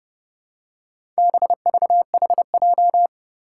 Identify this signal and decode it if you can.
Morse code, heard as a keyed tone, about 30 wpm, 700 Hz. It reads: BVHJ
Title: Four Challenging Letter-Number Combinations 30wpm